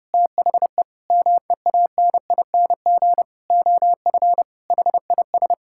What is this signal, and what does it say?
Morse code, heard as a keyed tone, about 30 wpm, 700 Hz. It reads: THE MEANING OF HIS
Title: Sentences from Top 1000 Words - Review Part 6 - 30wpm